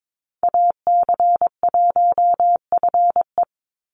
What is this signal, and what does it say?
Morse code, heard as a keyed tone, about 22 wpm, 700 Hz. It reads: AC1FE
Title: Call Signs - 22wpm